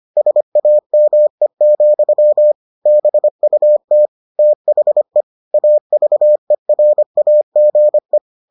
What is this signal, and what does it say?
Morse code, heard as a keyed tone, about 25 wpm, 600 Hz. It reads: SAME, BUT THE AVERAGE